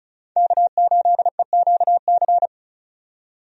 Morse code, heard as a keyed tone, about 36 wpm, 700 Hz. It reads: K8EQC
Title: USA Call Signs - 5 Characters - 35wpm